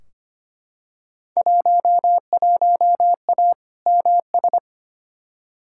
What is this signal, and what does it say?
Morse code, heard as a keyed tone, about 25 wpm, 700 Hz. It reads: 11A MS
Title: Field Day Exchanges Encore - 25wpm